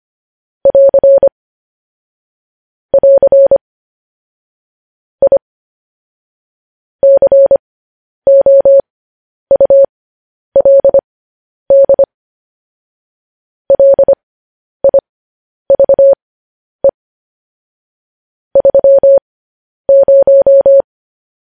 Morse code, heard as a keyed tone, about 25 wpm, spaced as for 10 wpm, 550 Hz. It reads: <AR> <AR> I COULD LIVE 30
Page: news headlines in morse code at 10 wpm